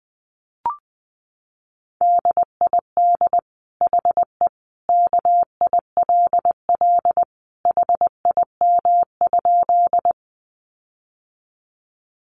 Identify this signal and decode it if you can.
Morse code, heard as a keyed tone, about 19 wpm, 700 Hz. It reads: DID HE KILL HIM?
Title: Sentences from Top 1400 Words 20wpm